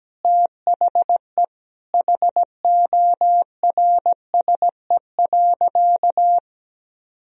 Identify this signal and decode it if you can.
Morse code, heard as a keyed tone, about 17 wpm, 700 Hz. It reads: THE HORSE.
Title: Sentences from Top 1000 Words - Review Part 1 - 17wpm